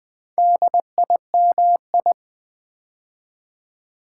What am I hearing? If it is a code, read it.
Morse code, heard as a keyed tone, about 20 wpm, 700 Hz. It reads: DIMI
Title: Build A Word - 2 Letters At A Time - 8-Letter Words - 20wpm